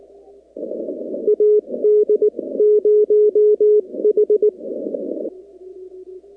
Morse code, heard as a keyed tone, about 19 wpm, spaced as for 16 wpm, 420 Hz. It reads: AD0H
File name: contests / 2019-cw-wpx-kt5j-k5pi- -2330 0-44